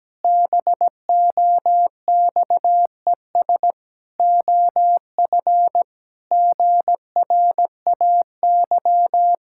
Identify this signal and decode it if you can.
Morse code, heard as a keyed tone, about 17 wpm, 700 Hz. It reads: BOXES OF GRAY
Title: Pangram Challenge v2 - 17wpm